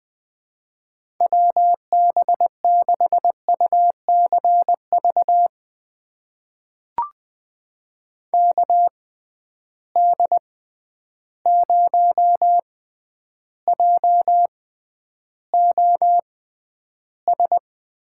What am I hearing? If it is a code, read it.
Morse code, heard as a keyed tone, about 20 wpm, spaced as for 10 wpm, 700 Hz. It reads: WB6UCV KD0JOS
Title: USA Call Signs - 6 Characters - 6x Character Spacing - 20wpm